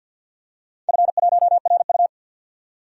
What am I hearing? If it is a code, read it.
Morse code, heard as a keyed tone, about 47 wpm, 700 Hz. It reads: F1RU
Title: DXCC-100 Call Signs - 4 Characters - 50wpm